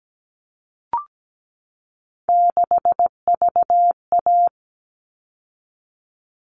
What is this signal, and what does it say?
Morse code, heard as a keyed tone, about 17 wpm, 700 Hz. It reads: 6VA
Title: All Cut Numbers - 3-Digit 17wpm